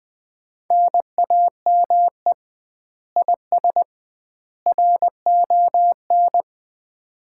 Morse code, heard as a keyed tone, about 20 wpm, 700 Hz. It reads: NAME IS RON
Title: Common QSO Phrases - 2x Word-Spacing - 20wpm